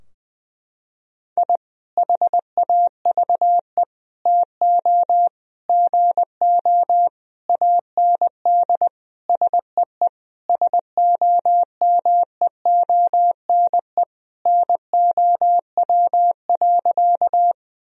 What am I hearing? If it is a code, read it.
Morse code, heard as a keyed tone, about 20 wpm, 700 Hz. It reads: I HAVE TO GO AND SEE SOMEONE NOW.
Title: Sentences from Top 1000 Words - Review Part 4 - 20wpm